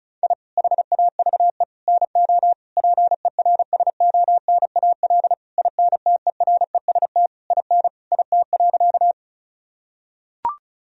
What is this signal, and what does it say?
Morse code, heard as a keyed tone, about 36 wpm, 700 Hz. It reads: I HAVE NO PERSONAL INTEREST IN IT.
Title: Sentences from Top 1000 Words - Review Part 4 - 35wpm